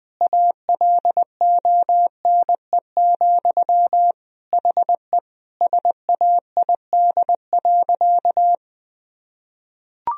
Morse code, heard as a keyed tone, about 20 wpm, 700 Hz. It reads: ALONE, HE SAID.
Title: Sentences from Top 1000 Words - Review Part 2 - 20wpm